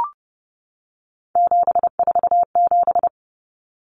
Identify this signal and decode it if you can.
Morse code, heard as a keyed tone, about 30 wpm, 700 Hz. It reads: 747
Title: Numbers - 3-Digit 30wpm